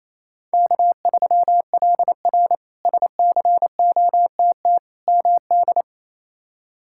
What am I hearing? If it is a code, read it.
Morse code, heard as a keyed tone, about 28 wpm, 700 Hz. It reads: K3LR SCOTT MD
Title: K1USN SST QSO Practice - 28wpm